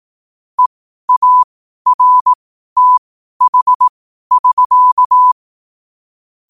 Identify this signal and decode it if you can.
Morse code, heard as a keyed tone, about 18 wpm, spaced as for 12 wpm, 1000 Hz. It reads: EARTH<SK>
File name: qotdfiles / 12wpm